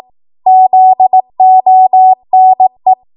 Morse code, heard as a keyed tone, about 18 wpm, 750 Hz. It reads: ZONE
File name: Archive / 18 WPM / 18WPM